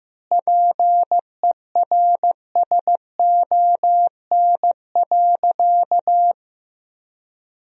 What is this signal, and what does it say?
Morse code, heard as a keyed tone, about 15 wpm, 700 Hz. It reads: PERSON.
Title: 2x Word-Spacing - Sentences from Top 500 Words - 15wpm